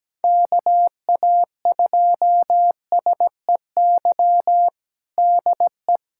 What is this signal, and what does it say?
Morse code, heard as a keyed tone, about 17 wpm, 700 Hz. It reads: KA2SEY DE
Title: USA Callsign DE Callsign 17wpm